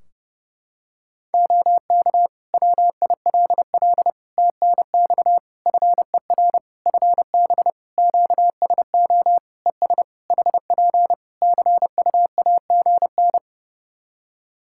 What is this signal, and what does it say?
Morse code, heard as a keyed tone, about 30 wpm, 700 Hz. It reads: OK WILL TNX FER FB QSO ES HP CUAGN
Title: Practice QSO - Exchange 4 - Ending - 30wpm